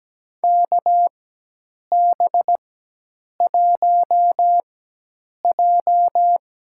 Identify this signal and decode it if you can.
Morse code, heard as a keyed tone, about 17 wpm, spaced as for 8 wpm, 700 Hz. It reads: KB1J
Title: USA Call Signs - 4x Character Spacing - 17wpm